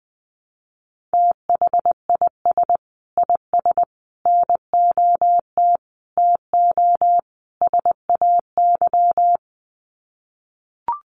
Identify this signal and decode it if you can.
Morse code, heard as a keyed tone, about 20 wpm, 700 Hz. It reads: THIS IS NOT TO SAY
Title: Sets of 5 Words Encore 20wpm